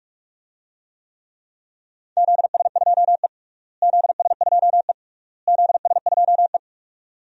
Morse code, heard as a keyed tone, about 46 wpm, 700 Hz. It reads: ZS2E ZS2E ZS2E
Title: DXCC-100 Call Signs - 4 Characters - Repeated 3x - 45wpm